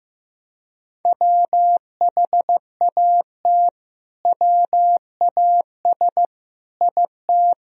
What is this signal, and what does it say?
Morse code, heard as a keyed tone, about 15 wpm, 700 Hz. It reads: WHAT WAS IT
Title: Sentences from Top 200 Words and Abbreviations 15wpm